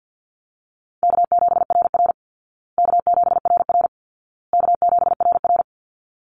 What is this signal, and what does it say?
Morse code, heard as a keyed tone, about 50 wpm, 700 Hz. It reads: K7RR K7RR K7RR
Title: Call Signs Repeated 2x - 50wpm